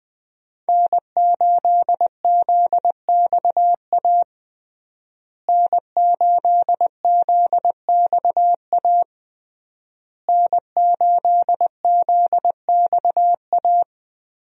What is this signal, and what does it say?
Morse code, heard as a keyed tone, about 20 wpm, 700 Hz. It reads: N8ZXA N8ZXA N8ZXA
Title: USA Call Signs - 5 Characters - Repeated 3x - 20wpm